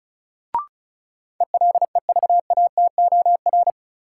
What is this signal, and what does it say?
Morse code, heard as a keyed tone, about 34 wpm, 700 Hz. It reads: ELEVATOR
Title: Sentences from Top 1600 Words - New Word Drill - 35wpm